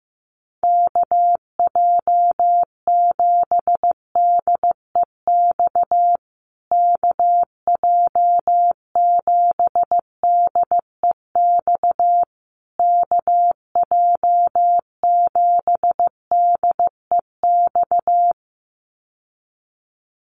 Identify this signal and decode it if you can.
Morse code, heard as a keyed tone, about 15 wpm, 700 Hz. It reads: KJ7DEX KJ7DEX KJ7DEX
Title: USA Call Signs Repeated 3x 15wpm